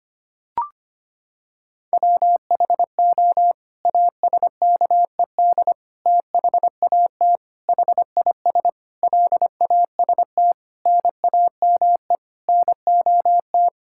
Morse code, heard as a keyed tone, about 25 wpm, 700 Hz. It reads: WHO ASKED THAT HIS LAST NAME NOT
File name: Sets of 9 Words - 25wpm